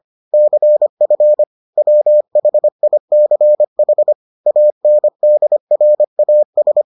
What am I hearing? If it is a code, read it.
Morse code, heard as a keyed tone, about 25 wpm, 600 Hz. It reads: CF WHICH ANDRAS